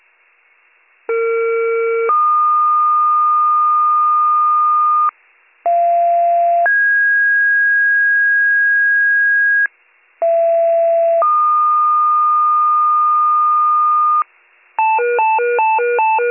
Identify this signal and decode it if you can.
Two-tone page, heard as a one-second tone, then a three-second tone, about 460 then 1220 Hz, about 690 then 1650 Hz, about 670 then 1220 Hz. It is heard from 1.1 to 14.2 s.